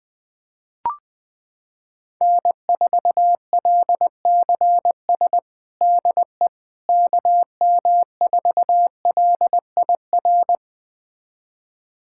Morse code, heard as a keyed tone, about 20 wpm, 700 Hz. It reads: N4LCS DE KM4LIR
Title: USA Callsign DE Callsign 20wpm